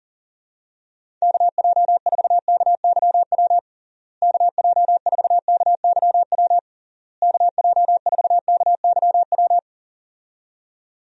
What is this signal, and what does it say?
Morse code, heard as a keyed tone, about 40 wpm, 700 Hz. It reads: KJ4KYW KJ4KYW KJ4KYW
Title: USA Call Signs - 6 Characters - Repeated 3x - 40wpm